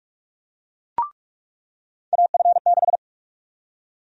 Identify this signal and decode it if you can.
Morse code, heard as a keyed tone, about 46 wpm, 700 Hz. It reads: AF6
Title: USA Call Sign Prefixes - 45wpm